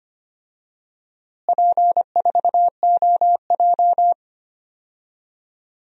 Morse code, heard as a keyed tone, about 25 wpm, 700 Hz. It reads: P4OJ
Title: DXCC-100 Call Signs - 4 Characters - Repeated 2x - 25wpm